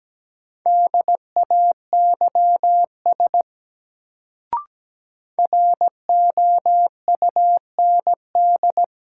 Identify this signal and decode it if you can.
Morse code, heard as a keyed tone, about 17 wpm, 700 Hz. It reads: DAYS ROUND
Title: Sets of 5 Words Encore - New Word Drill - 17wpm